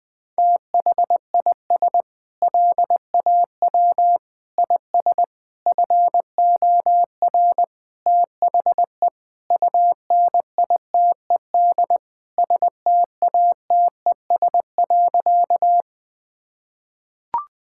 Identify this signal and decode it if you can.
Morse code, heard as a keyed tone, about 20 wpm, 690 Hz. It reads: THIS LAW IS FOR THE UNITED STATES.
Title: Sentences with 4-Word Phrases - Spoken Beginning - 20wpm